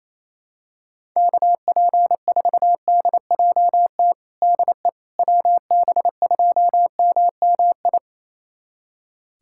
Morse code, heard as a keyed tone, about 28 wpm, 700 Hz. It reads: KP4DJT DE WB2MMI
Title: USA Callsign DE Callsign - 6 Characters - 28wpm